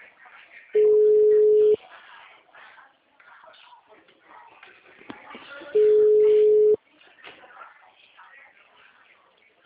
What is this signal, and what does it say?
Ringback tone (Europe, 425 Hz), with speech.
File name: out- -2020- - - .4403